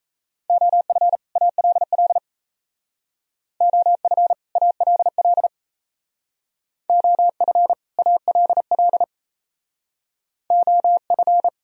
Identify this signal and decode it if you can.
Morse code, heard as a keyed tone, about 37 wpm, 700 Hz. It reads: OF ALL OF ALL OF ALL OF